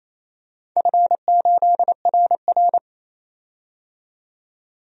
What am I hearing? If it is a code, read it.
Morse code, heard as a keyed tone, about 28 wpm, 700 Hz. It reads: F8RR
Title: DXCC-100 Call Signs - 4 Characters - 28wpm